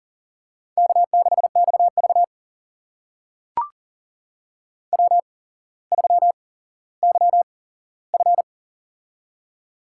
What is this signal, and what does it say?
Morse code, heard as a keyed tone, about 40 wpm, spaced as for 17 wpm, 700 Hz. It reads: K6XV W3YF
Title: USA Call Signs - 4 Characters - 8x Character Spacing - 40wpm